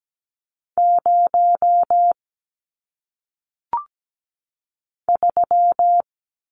Morse code, heard as a keyed tone, about 17 wpm, 700 Hz. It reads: 0 3
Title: Numbers - 1-Digit 17wpm